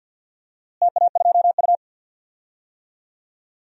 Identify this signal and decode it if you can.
Morse code, heard as a keyed tone, about 48 wpm, 700 Hz. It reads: TA2U
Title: DXCC-100 Call Signs - 4 Characters - Repeated 2x - 50wpm